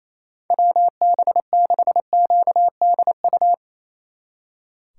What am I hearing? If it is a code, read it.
Morse code, heard as a keyed tone, about 35 wpm, 700 Hz. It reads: WB6QDU